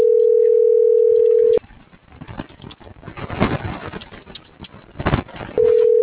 Ringback tone (North America), 440 + 480 Hz.